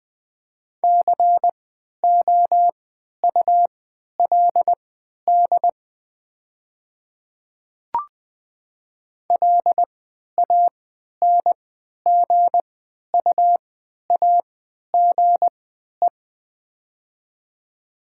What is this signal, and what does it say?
Morse code, heard as a keyed tone, about 20 wpm, spaced as for 11 wpm, 700 Hz. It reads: COULD LANGUAGE
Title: 3x Character Spacing - Top 500 Words - 20wpm